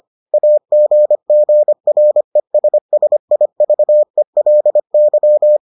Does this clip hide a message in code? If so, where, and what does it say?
Yes: Morse code from 0.3 to 5.6 s, heard as a keyed tone, about 25 wpm, 600 Hz. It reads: AGGRESSIVELY